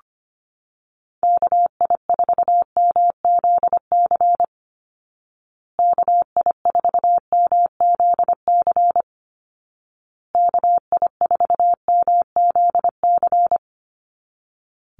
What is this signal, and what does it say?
Morse code, heard as a keyed tone, about 25 wpm, 700 Hz. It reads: KI4MZC KI4MZC KI4MZC
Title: Call Signs Repeated 2x - 25wpm